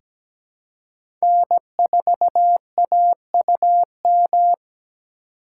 Morse code, heard as a keyed tone, about 17 wpm, 700 Hz. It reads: N4AUM
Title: USA Call Signs - 5 Characters - 4x Character Spacing - 17wpm